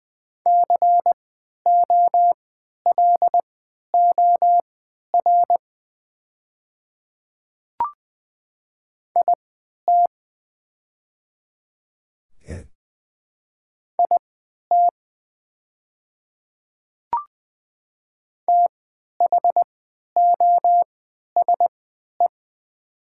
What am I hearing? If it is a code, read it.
Morse code, heard as a keyed tone, about 20 wpm, spaced as for 11 wpm, 700 Hz. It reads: COLOR IT IT THOSE
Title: 3x Character Spacing - Top 300 Words - 20wpm